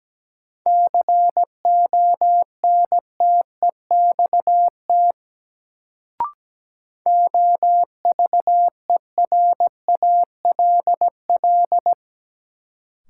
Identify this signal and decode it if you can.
Morse code, heard as a keyed tone, about 17 wpm, 700 Hz. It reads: CONTEXT OVERALL
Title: Sentences from Top 1000 Words - Review Part 7 - New Word Drill - 17wpm